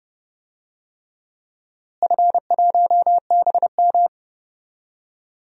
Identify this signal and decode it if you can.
Morse code, heard as a keyed tone, about 31 wpm, 700 Hz. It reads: F1BM
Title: DXCC-100 Call Signs - 4 Characters - Repeated 3x - 30wpm